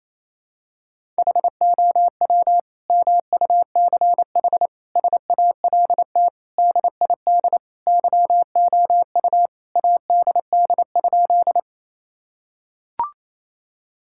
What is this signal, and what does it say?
Morse code, heard as a keyed tone, about 28 wpm, 700 Hz. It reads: HOW MUCH SALT DID YOU ADD?
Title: Sentences from Top 1000 Words - Review Part 5 - 28wpm